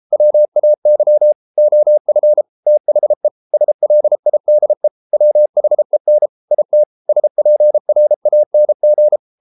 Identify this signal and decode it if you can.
Morse code, heard as a keyed tone, about 33 wpm, 600 Hz. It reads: WAY OF THE SLIDE WHEN IT SPRANG